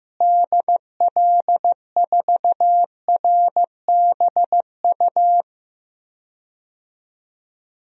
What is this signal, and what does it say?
Morse code, heard as a keyed tone, about 16 wpm, 700 Hz. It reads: DL4RBU
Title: DXCC-100 Call Signs - 6 Characters - Repeated 2x - 15wpm